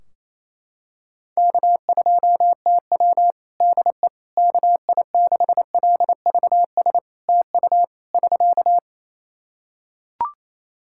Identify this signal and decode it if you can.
Morse code, heard as a keyed tone, about 28 wpm, 700 Hz. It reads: K2TW DE KI6LVS TU <SK>
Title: Practice QSO - Exchange 4 - Ending - 28wpm